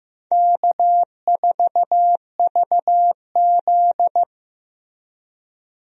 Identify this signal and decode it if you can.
Morse code, heard as a keyed tone, about 15 wpm, 700 Hz. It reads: K4VZ
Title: USA Call Signs - 4 Characters - Repeated 3x - 15wpm